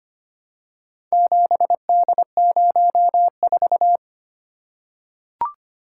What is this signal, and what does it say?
Morse code, heard as a keyed tone, about 25 wpm, 700 Hz. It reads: 7D04
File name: practice / 25wpm / Four Letter-Number Combinations - 25wpm